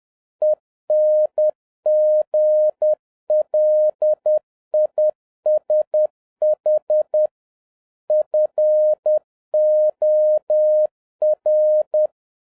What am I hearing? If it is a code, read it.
Morse code, heard as a keyed tone, about 10 wpm, 610 Hz. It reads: ENGLISH FOR